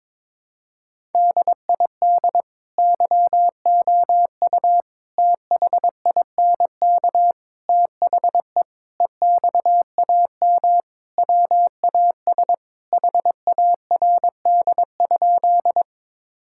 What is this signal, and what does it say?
Morse code, heard as a keyed tone, about 22 wpm, 700 Hz. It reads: DID YOU THINK THE EXAM WAS HARD?
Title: Sentences from Top 1000 Words - Review Part 5 - 22wpm